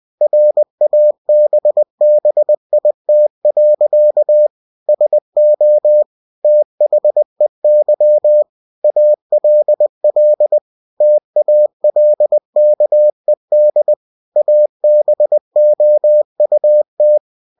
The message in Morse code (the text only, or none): RABBIT. SO THEY ALL TALKED ABOUT